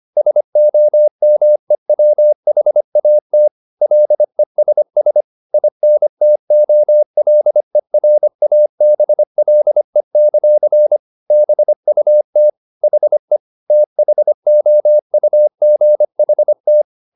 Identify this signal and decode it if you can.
Morse code, heard as a keyed tone, about 25 wpm, 600 Hz. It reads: SOMEWHAT LESS INTOLERABLE; BUT HE THOUGHT
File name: the old man and the sea part 1 25wpm